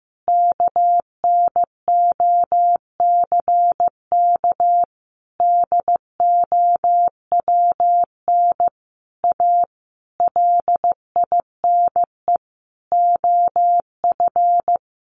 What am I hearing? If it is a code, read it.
Morse code, heard as a keyed tone, about 15 wpm, 700 Hz. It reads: KNOCK DOWN A LINE OF
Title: A Sound Of Thunder 15wpm